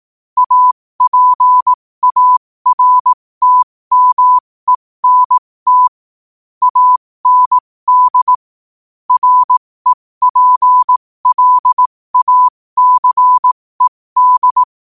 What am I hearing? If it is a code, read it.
Morse code, heard as a keyed tone, about 18 wpm, spaced as for 15 wpm, 1000 Hz. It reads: APARTMENT AND REPLACED
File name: qotdfiles / 15wpm